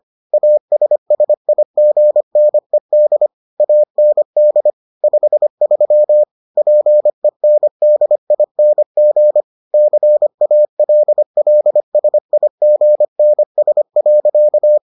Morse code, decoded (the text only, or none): ASSIGNED AND 53 PENDING CALLSIGNS.